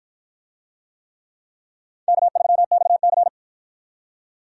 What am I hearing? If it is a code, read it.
Morse code, heard as a keyed tone, about 53 wpm, 700 Hz. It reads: K3XC